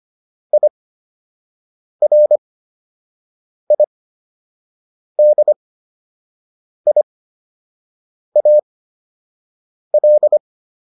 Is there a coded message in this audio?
Yes: Morse code from 0.5 to 10.4 s, heard as a keyed tone, about 25 wpm, spaced as for 6 wpm, 600 Hz. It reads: IRIDIAL